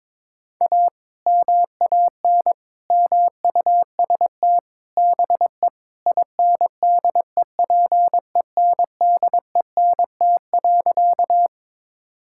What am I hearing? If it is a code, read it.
Morse code, heard as a keyed tone, about 22 wpm, 700 Hz. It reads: A MAN MUST BE INDEPENDENT.
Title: Sentences from Top 1000 Words - Review Part 6 - 22wpm